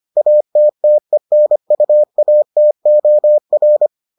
Morse code, heard as a keyed tone, about 25 wpm, 600 Hz. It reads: ATTENUATOR